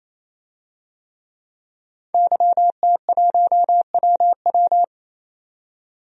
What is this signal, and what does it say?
Morse code, heard as a keyed tone, about 28 wpm, 700 Hz. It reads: YT1WW
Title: DXCC-100 Call Signs - 5 Characters - 28wpm